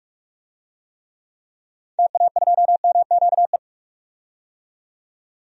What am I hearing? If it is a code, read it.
Morse code, heard as a keyed tone, about 46 wpm, 700 Hz. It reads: TA2MQE
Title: DXCC-100 Call Signs - 6 Characters - 45wpm